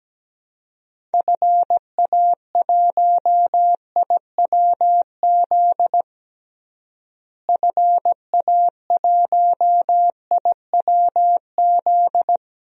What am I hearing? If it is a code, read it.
Morse code, heard as a keyed tone, about 17 wpm, 700 Hz. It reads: FA1IWZ FA1IWZ
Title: DXCC-100 Call Signs - 6 Characters - Repeated 3x - 17wpm